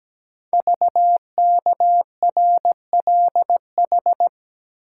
Morse code, heard as a keyed tone, about 17 wpm, 700 Hz. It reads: VKRLH
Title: Code Groups - 5 Letters - 17wpm